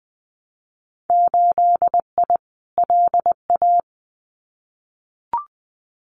Morse code, heard as a keyed tone, about 20 wpm, 700 Hz. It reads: 8I LA
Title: Winter Field Day Exchanges Encore 20wpm